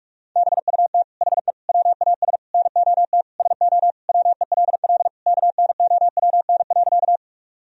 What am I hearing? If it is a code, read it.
Morse code, heard as a keyed tone, about 43 wpm, 700 Hz. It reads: BUT HE WAS NOT SO WELL KNOWN.
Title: Sentences from Top 1000 Words - Review Part 3 - 45wpm